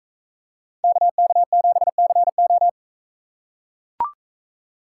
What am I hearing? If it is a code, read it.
Morse code, heard as a keyed tone, about 43 wpm, 700 Hz. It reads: KK7CO